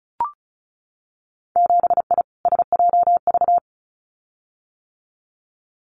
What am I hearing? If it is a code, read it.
Morse code, heard as a keyed tone, about 36 wpm, 700 Hz. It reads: 7I SJV
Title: Winter Field Day Exchanges Encore 35wpm